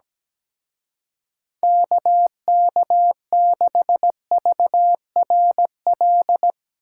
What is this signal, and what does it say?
Morse code, heard as a keyed tone, about 18 wpm, 700 Hz. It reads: KK6VRL
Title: Speed Racing - USA 2x3 Call Signs 17wpm